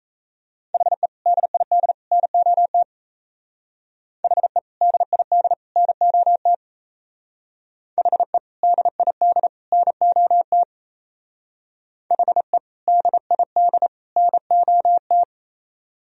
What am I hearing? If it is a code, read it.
Morse code, heard as a keyed tone, about 33 wpm, 700 Hz. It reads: HE DID NOT HE DID NOT HE DID NOT HE DID NOT
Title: Speed-Racing - Sets of 3 Words 28wpm